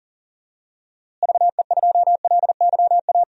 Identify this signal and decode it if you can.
Morse code, heard as a keyed tone, about 40 wpm, 700 Hz. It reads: VE2LYA
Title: DXCC-100 Call Signs - 6 Characters - Repeated 2x - 40wpm